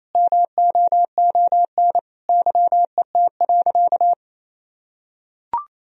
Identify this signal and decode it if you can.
Morse code, heard as a keyed tone, about 28 wpm, 700 Hz. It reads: MOON YET.
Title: Sentences from Top 1000 Words - Review Part 3 - 28wpm